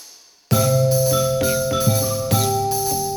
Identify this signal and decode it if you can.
There is no code.